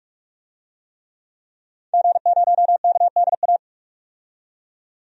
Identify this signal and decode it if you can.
Morse code, heard as a keyed tone, about 46 wpm, 700 Hz. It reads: G0KDA
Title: DXCC-100 Call Signs - 5 Characters - 45wpm